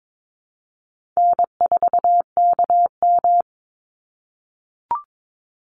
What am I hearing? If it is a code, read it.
Morse code, heard as a keyed tone, about 22 wpm, 700 Hz. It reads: N4KM
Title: Easy Call Signs - 22wpm